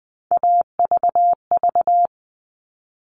Morse code, heard as a keyed tone, about 20 wpm, 700 Hz. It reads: AVV
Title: All Cut Numbers - 3-Digit 20wpm